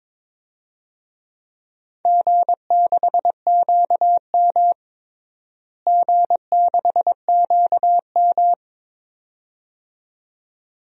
Morse code, heard as a keyed tone, about 22 wpm, 700 Hz. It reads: G6QM G6QM
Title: DXCC-100 Call Signs - 4 Characters - Repeated 2x - 22wpm